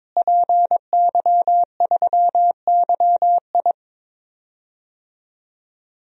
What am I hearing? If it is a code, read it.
Morse code, heard as a keyed tone, about 22 wpm, 700 Hz. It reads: PY3YI
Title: DXCC-100 Call Signs - 5 Characters - Repeated 3x - 22wpm